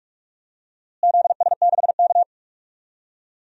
Morse code, heard as a keyed tone, about 46 wpm, 700 Hz. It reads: ZS6K
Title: DXCC-100 Call Signs - 4 Characters - Repeated 2x - 45wpm